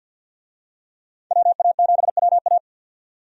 Morse code, heard as a keyed tone, about 50 wpm, 700 Hz. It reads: WA7WA
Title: USA Call Signs - 5 Characters - 50wpm